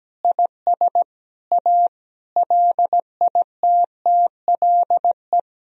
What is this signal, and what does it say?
Morse code, heard as a keyed tone, about 17 wpm, 700 Hz. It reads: IS A LITTLE
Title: Sentences from Top 1000 Words - Review Part 1 - 17wpm